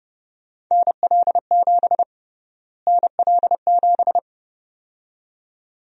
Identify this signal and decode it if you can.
Morse code, heard as a keyed tone, about 32 wpm, 700 Hz. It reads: NL7 NL7
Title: USA Call Sign Prefixes - Repeated 2x - 30wpm